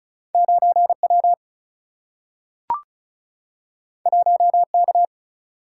Morse code, heard as a keyed tone, about 36 wpm, 700 Hz. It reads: 9 W 1 K
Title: Single-Letter Single-Number 35wpm